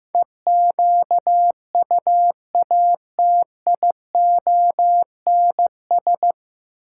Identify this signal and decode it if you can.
Morse code, heard as a keyed tone, about 15 wpm, 700 Hz. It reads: EQUATIONS
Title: Sets of 5 Words - New Word Drill - 15wpm